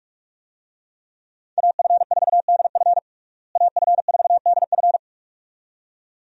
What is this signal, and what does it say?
Morse code, heard as a keyed tone, about 43 wpm, 700 Hz. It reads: AF4DF AF4DF
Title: USA Call Signs - 5 Characters - Repeated 2x - 45wpm